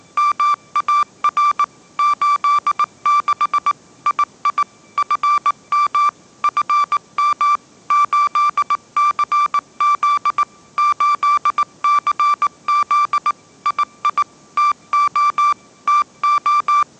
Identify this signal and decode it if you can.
Morse code, heard as a keyed tone, about 18 wpm, 1170 Hz. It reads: MAR 86 II FM FM 8CZ 8CZ II TO TO
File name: 20wpm